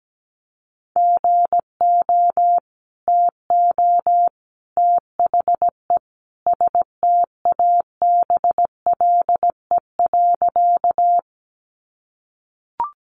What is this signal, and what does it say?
Morse code, heard as a keyed tone, about 17 wpm, 700 Hz. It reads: GO TO THE STABLE.
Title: Sentences from Top 1500 Words 17wpm